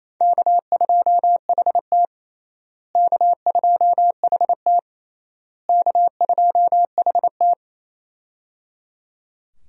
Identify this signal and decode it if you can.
Morse code, heard as a keyed tone, about 28 wpm, 700 Hz. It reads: K2HT K2HT K2HT
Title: USA Call Signs - 4 Characters - Repeated 3x - 28wpm